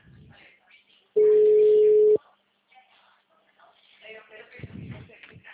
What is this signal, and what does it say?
Ringback tone (Europe, 425 Hz), with speech.